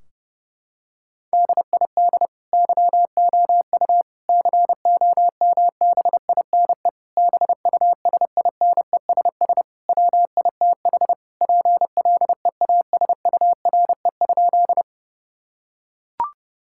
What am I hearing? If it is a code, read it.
Morse code, heard as a keyed tone, about 30 wpm, 700 Hz. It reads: DID YOU COMBINE BUSINESS WITH PLEASURE?
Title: Sentences from Top 1600 Words 30wpm